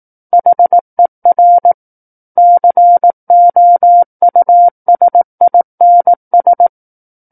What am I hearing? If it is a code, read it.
Morse code, heard as a keyed tone, about 18 wpm, 700 Hz. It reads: HER COUSINS
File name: 18 WPM